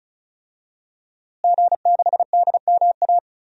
Morse code, heard as a keyed tone, about 34 wpm, 700 Hz. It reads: G6DMA
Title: DXCC-100 Call Signs - 5 Characters - Repeated 3x - 35wpm